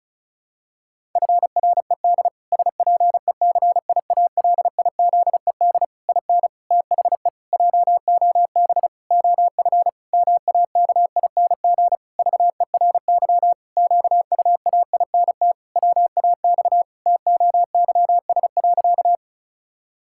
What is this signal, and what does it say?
Morse code, heard as a keyed tone, about 36 wpm, 700 Hz. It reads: FRED SPECIALIZED IN THE JOB OF MAKING VERY QUAINT WAX TOYS.
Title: Pangram Challenge v2 - 35wpm